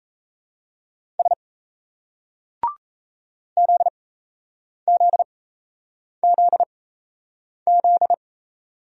Morse code, heard as a keyed tone, about 35 wpm, spaced as for 8 wpm, 700 Hz. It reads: S ZZZZ